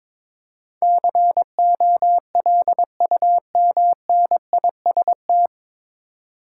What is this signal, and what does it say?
Morse code, heard as a keyed tone, about 22 wpm, 700 Hz. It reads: COLUMNIST
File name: 9 Letter Words - First 3 Letters Given - 22wpm